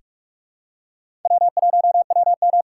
Morse code, heard as a keyed tone, about 46 wpm, 700 Hz. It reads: W1WM
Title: USA Call Signs - 4 Characters - 3x Character Spacing - 45wpm